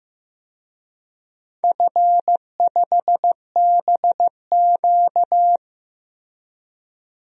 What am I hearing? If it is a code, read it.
Morse code, heard as a keyed tone, about 15 wpm, 700 Hz. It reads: F5BQ
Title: DXCC-100 Call Signs - 4 Characters - Repeated 2x - 15wpm